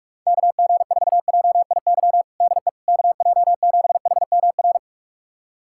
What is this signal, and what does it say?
Morse code, heard as a keyed tone, about 46 wpm, 700 Hz. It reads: KG4JIY DE KJ7HMR
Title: USA Callsign DE Callsign 45wpm